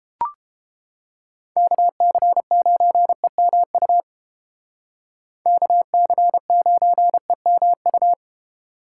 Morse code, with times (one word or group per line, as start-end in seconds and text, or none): 1.6-4.0 KC9EMU
5.5-8.1 KC9EMU